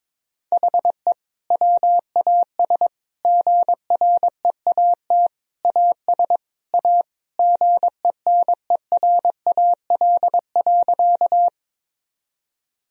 Morse code, heard as a keyed tone, about 22 wpm, 700 Hz. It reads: HE WAS GREAT AS A GENERAL.
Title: Sentences from Top 1000 Words - Review Part 3 - 22wpm